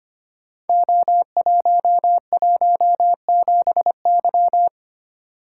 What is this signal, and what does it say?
Morse code, heard as a keyed tone, about 25 wpm, 700 Hz. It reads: O117Y
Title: Code Groups - 5 Letter:Numbers - 25wpm